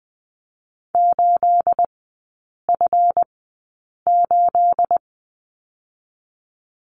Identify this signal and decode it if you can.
Morse code, heard as a keyed tone, about 20 wpm, spaced as for 8 wpm, 700 Hz. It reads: 8F8
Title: 212 Lesson 34e - Number 8 - Three Characters 20wpm